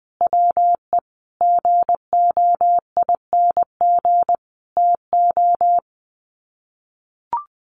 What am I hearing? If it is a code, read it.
Morse code, heard as a keyed tone, about 20 wpm, 700 Hz. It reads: WE GOING TO
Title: Sets of 5 Words Encore 20wpm